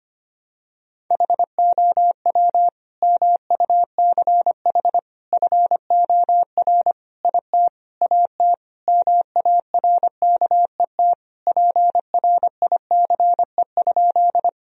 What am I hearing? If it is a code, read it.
Morse code, heard as a keyed tone, about 25 wpm, 700 Hz. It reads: HOW MUCH FOR IT AT MARKET PRICE?
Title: Sentences from Top 1000 Words - Review Part 3 - 25wpm